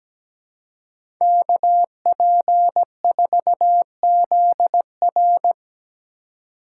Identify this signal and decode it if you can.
Morse code, heard as a keyed tone, about 17 wpm, 700 Hz. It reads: KP4ZR
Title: USA Call Signs - 5 Characters - 3x Character Spacing - 17wpm